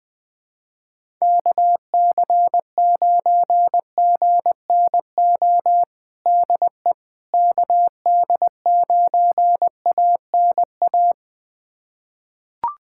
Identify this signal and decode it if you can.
Morse code, heard as a keyed tone, about 21 wpm, 700 Hz. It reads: KC9GNO DE KD9ANA
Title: USA Callsign DE Callsign - 6 Characters - 20wpm